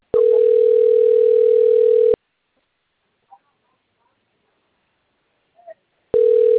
Ringback tone (North America), 440 + 480 Hz.